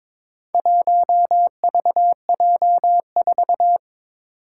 Morse code, heard as a keyed tone, about 23 wpm, 700 Hz. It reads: 1VJ4
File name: practice / 22wpm / Four Letter-Number Combinations - 22wpm